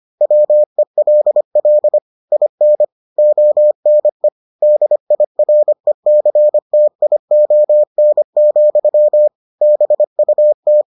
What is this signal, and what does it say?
Morse code, heard as a keyed tone, about 25 wpm, 600 Hz. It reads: WELL IN ONE DIRECTION, BUT